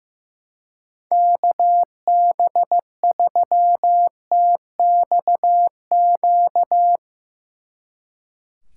Morse code, heard as a keyed tone, about 16 wpm, 700 Hz. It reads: KB3TXQ
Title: Speed Racing - USA 2x3 Call Signs 15wpm